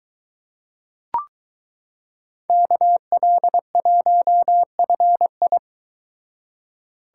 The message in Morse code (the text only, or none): KL1FI